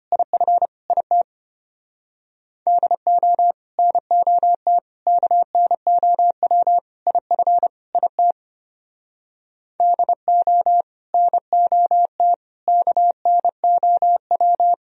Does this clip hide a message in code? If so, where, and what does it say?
Yes: Morse code from 0.1 to 14.7 s, heard as a keyed tone, about 33 wpm, 700 Hz. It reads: IF IT DO NOT KNOW IF IT DO NOT KNOW